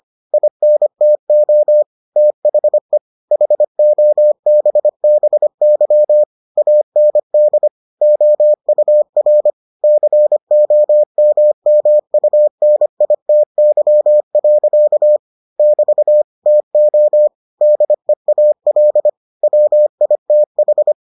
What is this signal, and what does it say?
Morse code, heard as a keyed tone, about 25 wpm, 600 Hz. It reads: INTO THE HOBBY AND OUR COMMUNITY. <BT> TO DEAL WITH